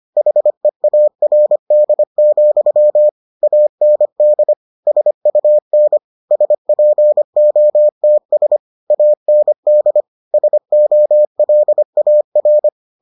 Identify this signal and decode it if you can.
Morse code, heard as a keyed tone, about 25 wpm, 600 Hz. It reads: HEARD, AND SUN SPOTS AND SOLAR